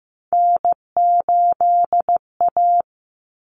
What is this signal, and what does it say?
Morse code, heard as a keyed tone, about 15 wpm, 700 Hz. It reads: N8A
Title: Common Cut Numbers - 3-Digit 15wpm